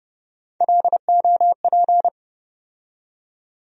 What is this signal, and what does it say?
Morse code, heard as a keyed tone, about 32 wpm, 700 Hz. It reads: LOP
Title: USA Call Sign Postfixes - 3 Letters 30wpm